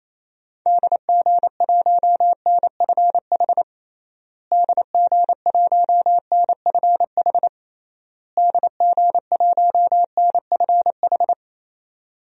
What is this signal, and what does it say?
Morse code, heard as a keyed tone, about 28 wpm, 700 Hz. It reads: DG1NFH DG1NFH DG1NFH
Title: DXCC-100 Call Signs - 6 Characters - Repeated 3x - 28wpm